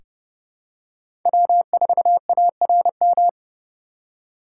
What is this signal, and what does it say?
Morse code, heard as a keyed tone, about 31 wpm, 700 Hz. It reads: W4ARM
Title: USA Call Signs - 5 Characters - 6x Character Spacing - 30wpm